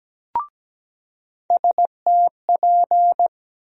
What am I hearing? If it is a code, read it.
Morse code, heard as a keyed tone, about 17 wpm, 700 Hz. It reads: STP
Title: USA Call Sign Postfixes - 3 Letters 17wpm